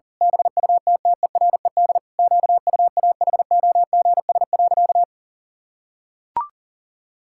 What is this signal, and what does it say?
Morse code, heard as a keyed tone, about 40 wpm, 700 Hz. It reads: BUTTERED QUAHOGS.
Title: Pangram Challenge v2 - 40wpm